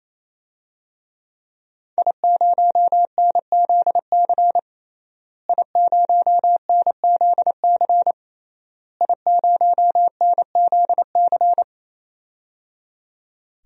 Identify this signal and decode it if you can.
Morse code, heard as a keyed tone, about 27 wpm, 700 Hz. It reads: I0NZC I0NZC I0NZC
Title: DXCC-100 Call Signs - 5 Characters - Repeated 3x - 28wpm